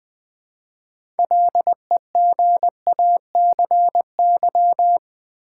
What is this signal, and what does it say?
Morse code, heard as a keyed tone, about 21 wpm, 690 Hz. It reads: LEGACY
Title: Build A Word - 2 Letters At A Time - 6-Letter Words - 20wpm